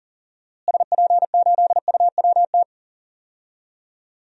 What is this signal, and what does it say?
Morse code, heard as a keyed tone, about 37 wpm, 700 Hz. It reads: SP8UWT
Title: DXCC-100 Call Signs - 6 Characters - Repeated 2x - 40wpm